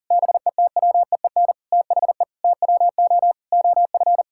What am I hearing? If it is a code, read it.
Morse code, heard as a keyed tone, about 40 wpm, 700 Hz. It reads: BETWEEN THE TWO OF
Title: Sentences from Top 1000 Words - Review Part 6 - 40wpm